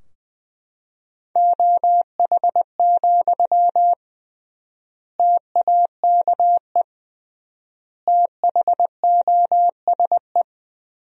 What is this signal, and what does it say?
Morse code, heard as a keyed tone, about 20 wpm, 700 Hz. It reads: OH, TAKE THOSE
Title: 3x Word-Spacing - Sentences from Top 500 Words - 20wpm